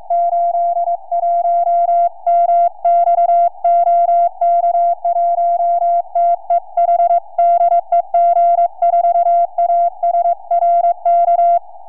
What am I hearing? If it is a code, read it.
Morse code, heard as a keyed tone, about 22 wpm, 690 Hz. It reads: 81MX OK1TEH DE G4ASR K
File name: g4asr 4m